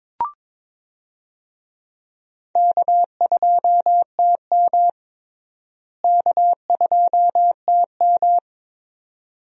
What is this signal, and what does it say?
Morse code, heard as a keyed tone, about 22 wpm, 700 Hz. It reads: K2TM K2TM
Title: USA Call Signs - 4 Characters - Repeated 2x - 22wpm